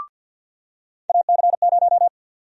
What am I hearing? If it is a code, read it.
Morse code, heard as a keyed tone, about 50 wpm, 700 Hz. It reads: AC0
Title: USA Call Sign Prefixes - 50wpm